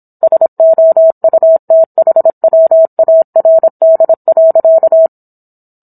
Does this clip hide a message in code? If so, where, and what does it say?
Yes: Morse code from 0.2 to 5.1 s, heard as a keyed tone, about 27 wpm, 650 Hz. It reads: SOUTHWARD.